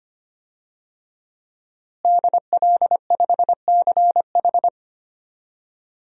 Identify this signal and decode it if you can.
Morse code, heard as a keyed tone, about 25 wpm, 700 Hz. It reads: DL5CH
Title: DXCC-100 Call Signs - 5 Characters - Repeated 3x - 25wpm